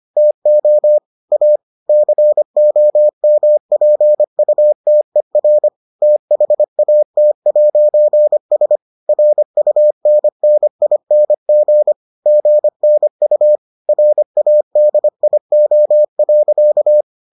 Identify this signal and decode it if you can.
Morse code, heard as a keyed tone, about 25 wpm, 600 Hz. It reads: TO A COMPUTER THAT'S RUNNING GNU RADIO.